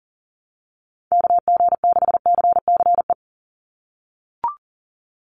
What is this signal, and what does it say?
Morse code, heard as a keyed tone, about 40 wpm, 700 Hz. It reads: KG6CCE
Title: USA Call Signs 40wpm